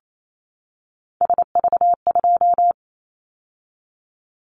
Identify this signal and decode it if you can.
Morse code, heard as a keyed tone, about 28 wpm, 700 Hz. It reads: SV2
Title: Callsign Prefixes for Top 50 DXCC - 28wpm